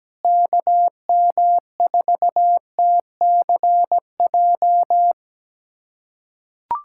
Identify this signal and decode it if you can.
Morse code, heard as a keyed tone, about 17 wpm, 700 Hz. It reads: KM4TCJ
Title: USA Callsign DE Callsign - 6 Characters - 17wpm